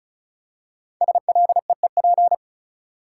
Morse code, heard as a keyed tone, about 36 wpm, 700 Hz. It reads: SLEEP
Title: Five Letter Words - 35wpm